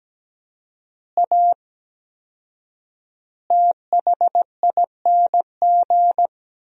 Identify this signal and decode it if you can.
Morse code, heard as a keyed tone, about 17 wpm, 700 Hz. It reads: A THING
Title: 4x Word-Spacing - Sentences from Top 300 Words - 17wpm